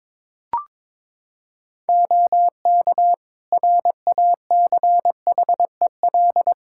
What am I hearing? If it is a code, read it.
Morse code, heard as a keyed tone, about 22 wpm, 700 Hz. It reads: OK RACHEL
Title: K1USN SST QSO Practice 22wpm